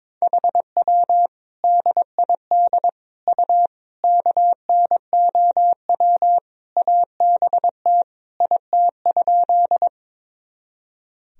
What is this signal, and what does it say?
Morse code, heard as a keyed tone, about 22 wpm, 700 Hz. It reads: HW DID U KNOW ABT IT?
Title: Sentences from Top 100 Words and Abbreviations 22wpm